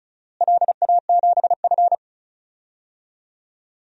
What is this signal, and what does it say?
Morse code, heard as a keyed tone, about 36 wpm, 700 Hz. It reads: LA7F
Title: DXCC-100 Call Signs - 4 Characters - Repeated 3x - 35wpm